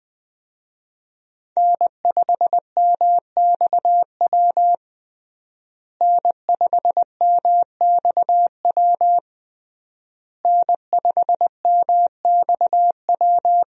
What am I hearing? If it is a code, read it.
Morse code, heard as a keyed tone, about 20 wpm, 700 Hz. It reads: N5MXW N5MXW N5MXW
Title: USA Call Signs - 5 Characters - Repeated 3x - 20wpm